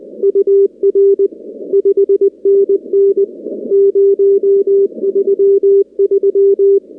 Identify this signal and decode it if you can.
Morse code, heard as a keyed tone, about 20 wpm, 400 Hz. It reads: UR 5NN 033
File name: contests / 2019-cw-wpx-kt5j-k5pi- -2330 0-13